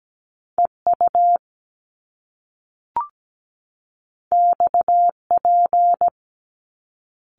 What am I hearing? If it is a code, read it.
Morse code, heard as a keyed tone, about 17 wpm, 700 Hz. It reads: EU XP
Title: Top Two Letter Combinations in Words 17wpm